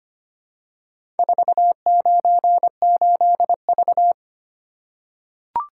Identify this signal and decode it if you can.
Morse code, heard as a keyed tone, about 25 wpm, 700 Hz. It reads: 498V
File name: practice / 25wpm / Four Letter-Number Combinations - 25wpm